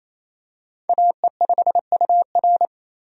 Morse code, heard as a keyed tone, about 28 wpm, 700 Hz. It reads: AE5UR
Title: USA Call Signs - 5 Characters - 28wpm